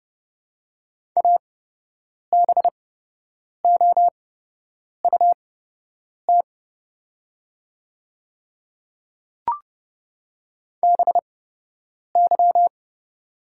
Morse code, heard as a keyed tone, about 30 wpm, spaced as for 8 wpm, 700 Hz. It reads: ABOUT BY
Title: 8x Character Spacing - Top 100 Words - 30wpm